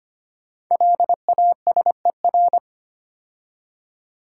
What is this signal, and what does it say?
Morse code, heard as a keyed tone, about 25 wpm, 700 Hz. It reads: LASER
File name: 5 Letter Words - First 3 Letters Given - 25wpm